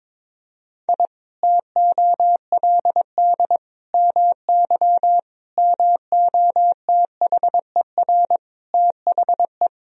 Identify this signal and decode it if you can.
Morse code, heard as a keyed tone, about 22 wpm, 700 Hz. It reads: I TOLD MY MOTHER THE
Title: Sentences from Top 1000 Words - Review Part 1 - 22wpm